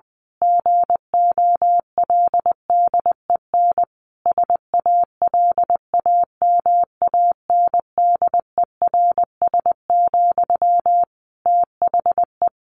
Morse code, heard as a keyed tone, about 20 wpm, 700 Hz. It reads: GOLDEN SALAMANDERS, THE
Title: A Sound Of Thunder 20wpm